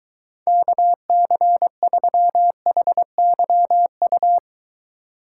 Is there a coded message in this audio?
Yes: Morse code from 0.5 to 4.4 s, heard as a keyed tone, about 23 wpm, 700 Hz. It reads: KC3HYU